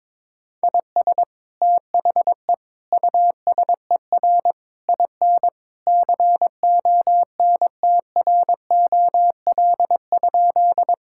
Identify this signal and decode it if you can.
Morse code, heard as a keyed tone, about 22 wpm, 700 Hz. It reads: IS THE USER IN CONTROL?
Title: Sentences from Top 1000 Words - Review Part 5 - 22wpm